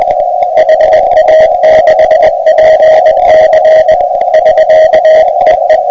Morse code, heard as a keyed tone, about 21 wpm, 630 Hz, with radio noise. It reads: HA6PC <SK> EE